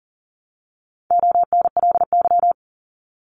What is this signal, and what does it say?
Morse code, heard as a keyed tone, about 40 wpm, 700 Hz. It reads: ONLY
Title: Top 200 Words 40wpm